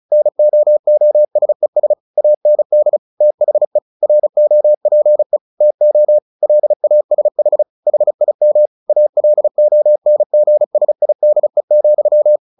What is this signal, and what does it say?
Morse code, heard as a keyed tone, about 34 wpm, 600 Hz. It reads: NOOSES AND THE ROPE TO LASH HIM ALONGSIDE,
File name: the old man and the sea 2 35wpm